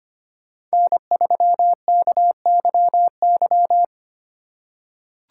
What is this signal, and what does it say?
Morse code, heard as a keyed tone, about 25 wpm, 700 Hz. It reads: N3KYY
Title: USA Call Signs - 5 Characters - 2x Character Spacing - 25wpm